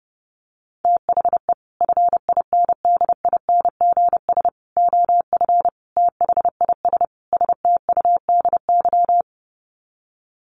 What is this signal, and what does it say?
Morse code, heard as a keyed tone, about 30 wpm, 700 Hz. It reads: THE FINDINGS OF THIS STUDY
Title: Sets of 5 Words Encore2 30wpm